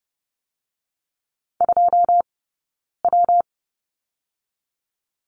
Morse code, heard as a keyed tone, about 30 wpm, 700 Hz. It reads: 2 W
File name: Lesson 19d - Letter w - Two Characters 30wpm